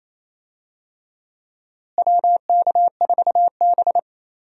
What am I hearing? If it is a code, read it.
Morse code, heard as a keyed tone, about 28 wpm, 700 Hz. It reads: WK4B
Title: USA Call Signs - 4 Characters - Repeated 3x - 28wpm